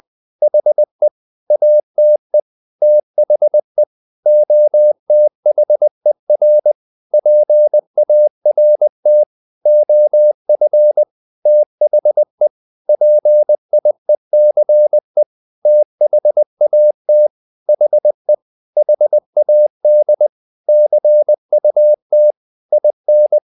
Morse code, heard as a keyed tone, about 20 wpm, 600 Hz. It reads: HE ATE THE OTHER PART OF THE PIECE THAT HE HAD CUT IN
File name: the old man and the sea / the old man and the sea part 1 20wpm